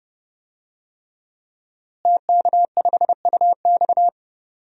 Morse code, heard as a keyed tone, about 32 wpm, 700 Hz. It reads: TK5UX
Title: DXCC-100 Call Signs - 5 Characters - Repeated 3x - 30wpm